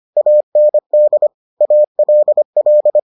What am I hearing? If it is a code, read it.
Morse code, heard as a keyed tone, about 25 wpm, 600 Hz. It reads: AND ALL